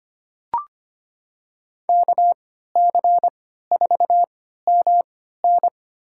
Morse code, heard as a keyed tone, about 25 wpm, spaced as for 14 wpm, 700 Hz. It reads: KC4MN
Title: USA Call Signs - 5 Characters - 3x Character Spacing - 25wpm